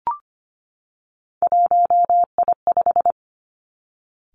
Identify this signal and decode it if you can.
Morse code, heard as a keyed tone, about 25 wpm, 700 Hz. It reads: 1I5
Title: Three Challenging Letter-Number Combinations 25wpm